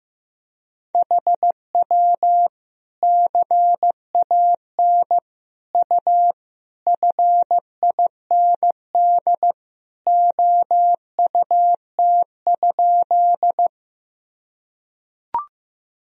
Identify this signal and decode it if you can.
Morse code, heard as a keyed tone, about 15 wpm, 700 Hz. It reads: HW CAN U FIND OUT?
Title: Sentences from Top 200 Words and Abbreviations 15wpm